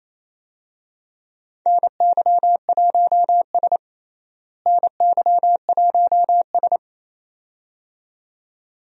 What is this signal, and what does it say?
Morse code, heard as a keyed tone, about 28 wpm, 700 Hz. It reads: NY1S NY1S
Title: USA Call Signs - 4 Characters - Repeated 2x - 28wpm